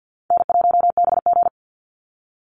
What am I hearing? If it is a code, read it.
Morse code, heard as a keyed tone, about 52 wpm, 700 Hz. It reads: N1BG
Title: Easy Call Signs Repeated 3x - 50wpm